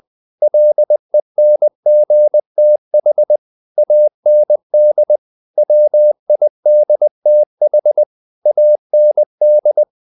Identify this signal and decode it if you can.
Morse code, heard as a keyed tone, about 20 wpm, 600 Hz. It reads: LENGTH AND WIDTH AND
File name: the old man and the sea part 2 20wpm